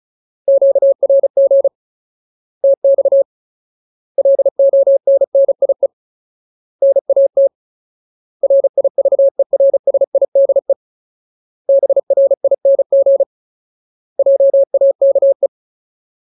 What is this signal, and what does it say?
Morse code, heard as a keyed tone, about 34 wpm, 550 Hz. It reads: QRG TX LONNIE NAT RIVERSIDE BRING JAKE
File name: Ultimate Word List / 35WPM Ultimate